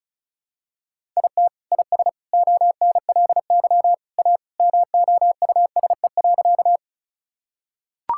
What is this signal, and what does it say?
Morse code, heard as a keyed tone, about 36 wpm, 700 Hz. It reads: IT IS ONLY A MOUSE.
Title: Sentences from Top 1000 Words - Review Part 5 - 35wpm